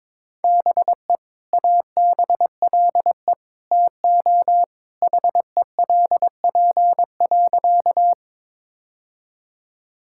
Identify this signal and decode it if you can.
Morse code, heard as a keyed tone, about 22 wpm, 700 Hz. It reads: BE ABLE TO HELP.
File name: practice / 22wpm / Sentences with 2-Word Phrases - 22wpm